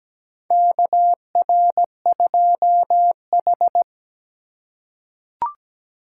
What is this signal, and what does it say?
Morse code, heard as a keyed tone, about 17 wpm, 700 Hz. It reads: KR2H
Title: USA Call Signs - 4 Characters - 6x Character Spacing - 17wpm